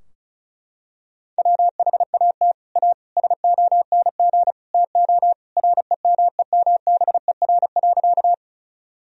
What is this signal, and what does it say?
Morse code, heard as a keyed tone, about 34 wpm, 700 Hz. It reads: WHAT A SONG TO REMEMBER.
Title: Sentences from Top 1000 Words - Review Part 2 - 35wpm